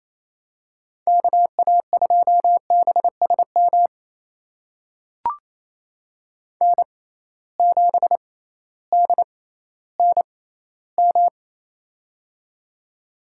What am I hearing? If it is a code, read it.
Morse code, heard as a keyed tone, about 28 wpm, spaced as for 15 wpm, 700 Hz. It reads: KA2BSM N7DNM
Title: USA Call Signs - 6x Character Spacing - 28wpm